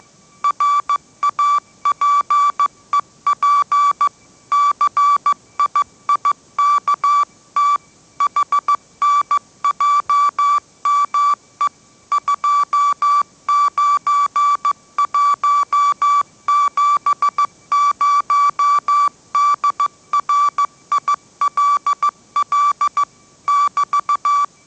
Morse code, heard as a keyed tone, about 15 wpm, 1160 Hz. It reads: RAPEP CIIKT HNJME 29170 DRILL <BT>
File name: mrx / mrx066 16wpm